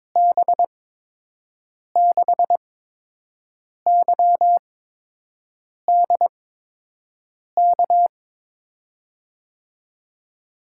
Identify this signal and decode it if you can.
Morse code, heard as a keyed tone, about 22 wpm, spaced as for 6 wpm, 700 Hz. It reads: B6YDK
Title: USA Call Signs - 6 Characters - 8x Character Spacing - 22wpm